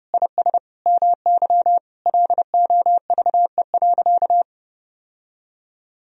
Morse code, heard as a keyed tone, about 32 wpm, 700 Hz. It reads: IS MY LOVE.
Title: Sentences from Top 1000 Words - Review Part 3 - 30wpm